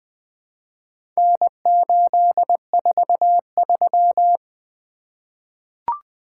Morse code, heard as a keyed tone, about 20 wpm, 700 Hz. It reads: N843
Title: Four Letter-Number Combinations - 20wpm